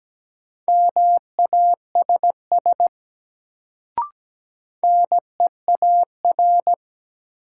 Morse code, heard as a keyed tone, about 17 wpm, 700 Hz. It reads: MASS NEAR
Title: Sets of 5 Words Encore - New Word Drill - 17wpm